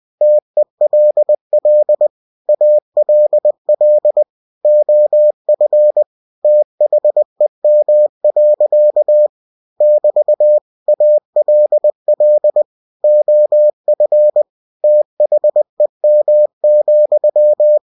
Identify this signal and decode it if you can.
Morse code, heard as a keyed tone, about 20 wpm, 600 Hz. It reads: TELL ALL OF THEM. <BT> ALL OF THEM,
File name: cw-books / winnie-the-pooh / winnie-20wpm-page-0105